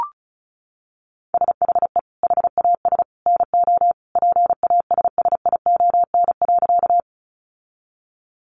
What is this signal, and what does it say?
Morse code, heard as a keyed tone, about 34 wpm, 700 Hz. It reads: SHE HAS NO PASSION.
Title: Sentences from Top 1300 Words 35wpm